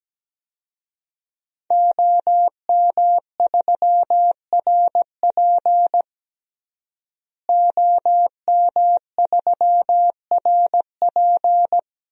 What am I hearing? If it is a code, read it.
Morse code, heard as a keyed tone, about 17 wpm, 700 Hz. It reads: OM3RP OM3RP
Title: DXCC-100 Call Signs - 5 Characters - Repeated 3x - 17wpm